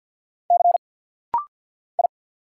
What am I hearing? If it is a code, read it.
Morse code, heard as a keyed tone, about 49 wpm, 700 Hz. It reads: C I
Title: Single Letter-Number Alt - Rapid-Fire 50wpm